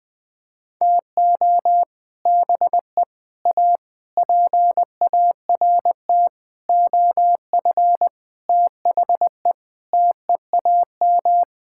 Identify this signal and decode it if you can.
Morse code, heard as a keyed tone, about 20 wpm, 700 Hz. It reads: TO BE A PART OF THE TEAM
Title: Sentences with 6-Word Phrases - 20wpm